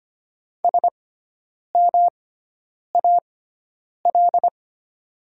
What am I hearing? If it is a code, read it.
Morse code, heard as a keyed tone, about 25 wpm, spaced as for 9 wpm, 700 Hz. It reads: SMAL
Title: 6x Character Spacing - Top 200 Words - 25wpm